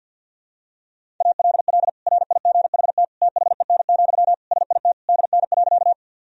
Morse code, heard as a keyed tone, about 51 wpm, 700 Hz. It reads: ALL RIGHT THEN, SIT DN.
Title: Sentences from Top 500 Words and Abbreviations 50wpm